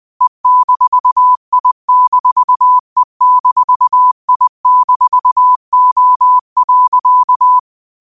Morse code, heard as a keyed tone, about 20 wpm, 1000 Hz. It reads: E-I-E-I-O.
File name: qotdfiles / 20wpm